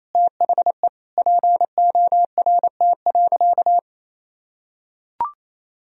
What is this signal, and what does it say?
Morse code, heard as a keyed tone, about 28 wpm, 700 Hz. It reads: THE PORT.
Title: Sentences from Top 1000 Words - Review Part 1 - 28wpm